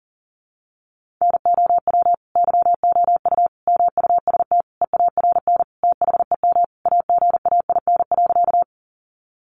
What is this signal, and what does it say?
Morse code, heard as a keyed tone, about 37 wpm, 700 Hz. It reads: NOW YOU MUST EARN THEM AGAIN.
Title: Sentences from Top 1500 Words 40wpm